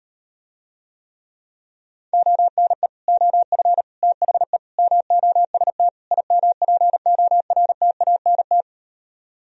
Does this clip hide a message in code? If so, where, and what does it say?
Yes: Morse code from 2.1 to 8.6 s, heard as a keyed tone, about 38 wpm, 700 Hz. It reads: ONE OF THE MOST IMPORTANT